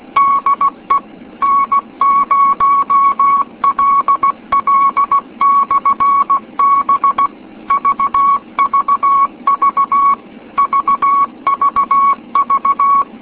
Morse code, heard as a keyed tone, about 16 wpm, 1110 Hz. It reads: DE N0LL/B VVV VVV